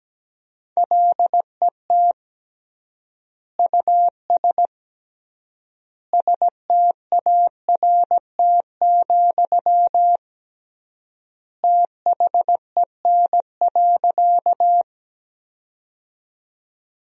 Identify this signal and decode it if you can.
Morse code, heard as a keyed tone, about 17 wpm, 700 Hz. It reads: LET US START, THEN.
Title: 3x Word-Spacing - Sentences from Top 300 Words - 17wpm